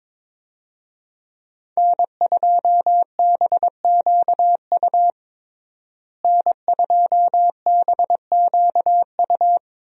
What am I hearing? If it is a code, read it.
Morse code, heard as a keyed tone, about 22 wpm, 700 Hz. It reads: N2BQU N2BQU
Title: USA Call Signs - 5 Characters - Repeated 3x - 22wpm